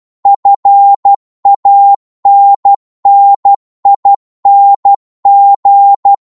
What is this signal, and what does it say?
Morse code, heard as a keyed tone, about 12 wpm, 800 Hz. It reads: FANNING